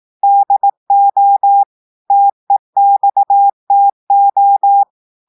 Morse code, heard as a keyed tone, about 18 wpm, 800 Hz. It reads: DO TEXTO
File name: audios / portugais / portugais texte en clair 1 vitesse 18 mots minutes